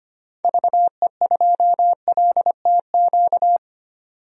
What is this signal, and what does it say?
Morse code, heard as a keyed tone, about 25 wpm, 700 Hz. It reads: VE2LTQ
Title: DXCC-100 Call Signs - 6 Characters - 25wpm